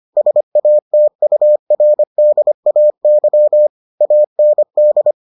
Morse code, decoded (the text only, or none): SATURDAY AND